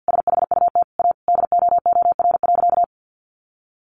Morse code, heard as a keyed tone, about 50 wpm, 700 Hz. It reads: SHUT A DOOR.
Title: A Sound Of Thunder 50wpm